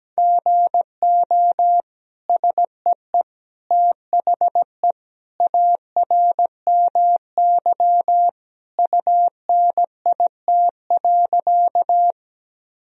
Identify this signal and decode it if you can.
Morse code, heard as a keyed tone, about 17 wpm, 700 Hz. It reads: GO SEE THE ARMY UNIT.
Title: Sentences from Top 1000 Words - Review Part 2 - 17wpm